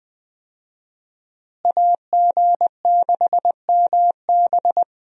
Morse code, heard as a keyed tone, about 20 wpm, 700 Hz. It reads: AG6MB
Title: USA Call Signs - 5 Characters - Repeated 2x - 20wpm